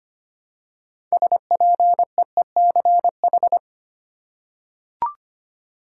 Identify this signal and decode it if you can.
Morse code, heard as a keyed tone, about 25 wpm, 700 Hz. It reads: SPEECH
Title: Six Letter Words - 25wpm